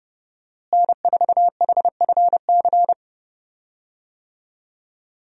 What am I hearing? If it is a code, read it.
Morse code, heard as a keyed tone, about 30 wpm, 700 Hz. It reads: N4HFC
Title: USA Call Signs - 5 Characters - Repeated 3x - 30wpm